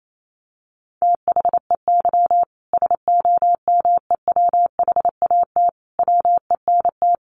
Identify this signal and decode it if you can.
Morse code, heard as a keyed tone, about 28 wpm, 700 Hz. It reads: THEY SOMEWHAT WENT
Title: Sentences from Top 1300 Words 28wpm